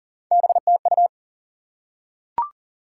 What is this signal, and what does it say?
Morse code, heard as a keyed tone, about 40 wpm, 700 Hz. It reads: BTU
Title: Top 180 QSO Elements 40wpm